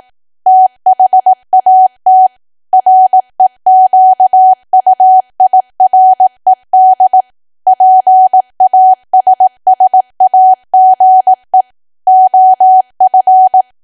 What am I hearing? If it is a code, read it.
Morse code, heard as a keyed tone, about 18 wpm, 750 Hz. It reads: THAT REQUIRED PASSAGE OF
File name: cwa / W1AW-Level2 / 18WPM